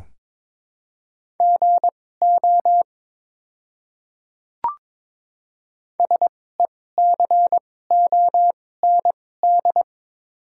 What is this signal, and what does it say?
Morse code, heard as a keyed tone, about 22 wpm, spaced as for 16 wpm, 700 Hz. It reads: GO SECOND
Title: 2x Character Spacing - Top 300 Words - 22wpm